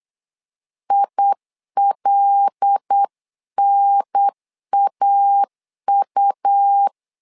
Morse code, heard as a keyed tone, about 8 wpm, 800 Hz. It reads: ILNAU
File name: LEZIONI / 5 LBVC / 5 LBV 8wpm 2837